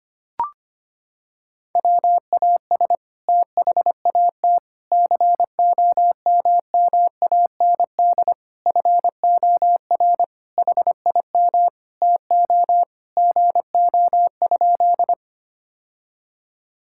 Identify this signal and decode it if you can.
Morse code, heard as a keyed tone, about 25 wpm, 700 Hz. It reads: WAS THAT COMMAND FOR HIM TO GO?
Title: Sentences from Top 1600 Words 25wpm